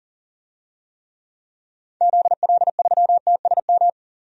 Morse code, heard as a keyed tone, about 40 wpm, 700 Hz. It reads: ZL3TSM
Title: DXCC-100 Call Signs - 6 Characters - 40wpm